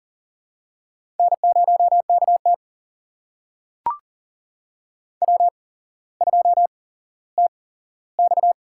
Morse code, heard as a keyed tone, about 42 wpm, spaced as for 17 wpm, 700 Hz. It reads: N0KT W2TX
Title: USA Call Signs - 4 Characters - 8x Character Spacing - 40wpm